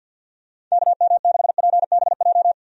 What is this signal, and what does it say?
Morse code, heard as a keyed tone, about 50 wpm, 700 Hz. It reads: KM6PBJ
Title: USA Call Signs - 6 Characters - Repeated 2x - 50wpm